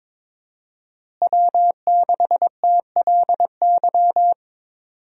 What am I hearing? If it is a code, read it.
Morse code, heard as a keyed tone, about 22 wpm, 700 Hz. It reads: W6TLY
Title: USA Call Signs - 5 Characters - Repeated 3x - 22wpm